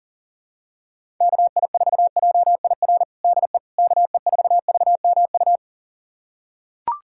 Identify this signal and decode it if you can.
Morse code, heard as a keyed tone, about 40 wpm, 700 Hz. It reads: KI4JIR DE KE4VMU
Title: USA Callsign DE Callsign - 6 Characters - 40wpm